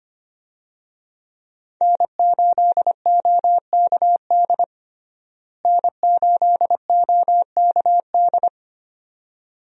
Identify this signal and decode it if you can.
Morse code, heard as a keyed tone, about 25 wpm, 700 Hz. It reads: N8OKD N8OKD
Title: USA Call Signs - 5 Characters - Repeated 2x - 25wpm